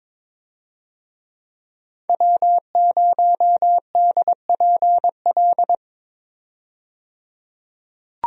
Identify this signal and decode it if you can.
Morse code, heard as a keyed tone, about 22 wpm, 700 Hz. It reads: W0DPL
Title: USA Call Signs - 5 Characters - 22wpm